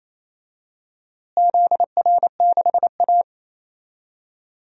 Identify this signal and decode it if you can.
Morse code, heard as a keyed tone, about 27 wpm, 700 Hz. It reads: ZR6A
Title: DXCC-100 Call Signs - 4 Characters - Repeated 2x - 28wpm